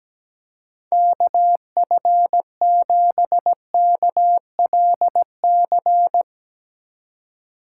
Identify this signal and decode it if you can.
Morse code, heard as a keyed tone, about 17 wpm, 700 Hz. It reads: KF7KLC
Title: Speed Racing - USA 2x3 Call Signs 17wpm